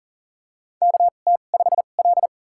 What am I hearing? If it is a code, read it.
Morse code, heard as a keyed tone, about 40 wpm, spaced as for 29 wpm, 700 Hz. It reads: KT5L
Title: USA Call Signs - 4 Characters - 2x Character Spacing - 40wpm